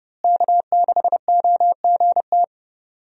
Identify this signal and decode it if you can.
Morse code, heard as a keyed tone, about 30 wpm, 700 Hz. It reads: K6OGT
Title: USA Call Signs - 5 Characters - Repeated 3x - 30wpm